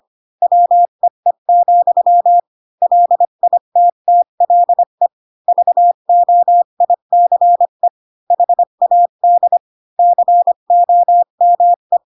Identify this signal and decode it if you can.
Morse code, heard as a keyed tone, about 25 wpm, 700 Hz. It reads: WEE, LITTLE VOICE HAD COME
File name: Pinocchio / Pinocchio ch1 20 25 ik1hge